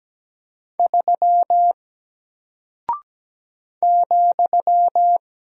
Morse code, heard as a keyed tone, about 17 wpm, 700 Hz. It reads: 3 ,
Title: Single Character (Letter:Number:Punctuation:Prosign) Alt 17wpm